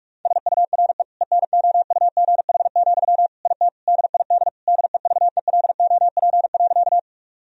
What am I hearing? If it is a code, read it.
Morse code, heard as a keyed tone, about 46 wpm, 700 Hz. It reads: SURE ENOUGH, IT DID DEVELOP.
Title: Sentences from Top 1000 Words - Review Part 2 - 45wpm